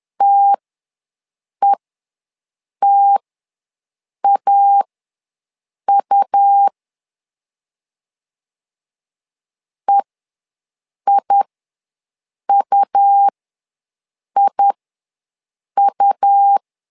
Morse code, heard as a keyed tone, about 11 wpm, spaced as for 6 wpm, 800 Hz. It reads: TETAU EIUIU
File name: LEZIONI / 2 UHDN / 2 U ws 10wpm 2017